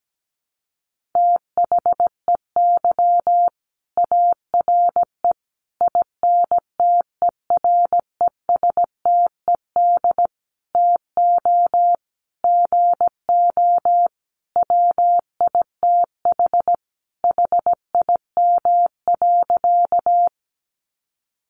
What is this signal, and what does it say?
Morse code, heard as a keyed tone, about 17 wpm, 700 Hz. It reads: THEY ARE INTERESTED TO GO WITH HIM.
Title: Sentences from Top 900 Words 17wpm